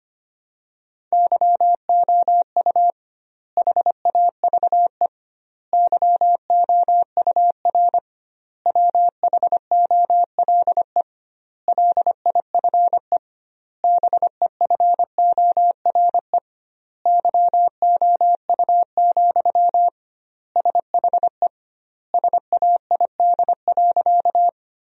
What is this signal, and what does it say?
Morse code, heard as a keyed tone, about 25 wpm, 700 Hz. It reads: YOU HAVE YOUR WHOLE LIFE BEFORE YOU, SHE SAID.
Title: 2x Word-Spacing - Sentences from Top 400 Words - 25wpm